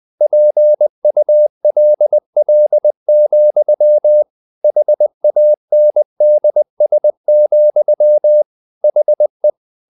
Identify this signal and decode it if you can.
Morse code, heard as a keyed tone, about 20 wpm, 600 Hz. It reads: PULL, HANDS, HE
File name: the old man and the sea part 2 20wpm